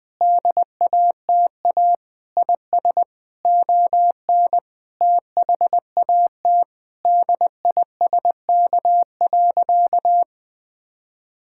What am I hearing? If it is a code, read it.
Morse code, heard as a keyed tone, about 20 wpm, 700 Hz. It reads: DATA IS ON THAT DISK.
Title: Sentences from Top 1000 Words - Review Part 6 - 20wpm